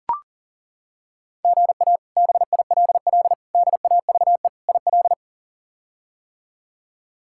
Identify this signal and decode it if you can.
Morse code, heard as a keyed tone, about 37 wpm, 700 Hz. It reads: GA BILL DAVE IL
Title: K1USN SST QSO Practice - 40wpm